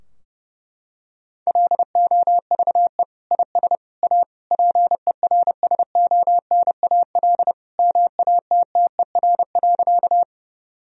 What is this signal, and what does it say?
Morse code, heard as a keyed tone, about 32 wpm, 700 Hz. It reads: LOVE IS A PERSONAL MATTER.
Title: Sentences from Top 1000 Words - Review Part 4 - 30wpm